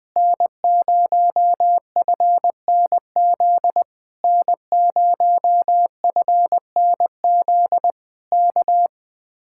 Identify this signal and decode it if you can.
Morse code, heard as a keyed tone, about 19 wpm, 700 Hz. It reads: N0FNZ N0FNZ K
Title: Speed-Racing Common QSO Phrases - 20wpm